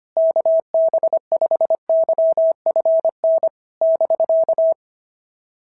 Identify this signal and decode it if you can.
Morse code, heard as a keyed tone, about 25 wpm, 650 Hz. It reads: KB5YFN <BK>
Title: SOTA QSO Practice 25wpm